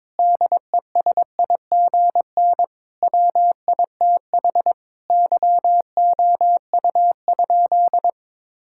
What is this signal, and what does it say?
Morse code, heard as a keyed tone, about 22 wpm, 700 Hz. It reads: DESIGN WITH YOU?
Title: Sentences from Top 1000 Words - Review Part 5 - 22wpm